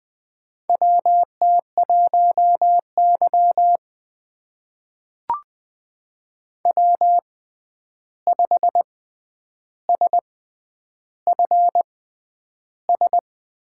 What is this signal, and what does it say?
Morse code, heard as a keyed tone, about 20 wpm, spaced as for 10 wpm, 700 Hz. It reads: WT1Y W5SFS
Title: USA Call Signs - 6x Character Spacing - 20wpm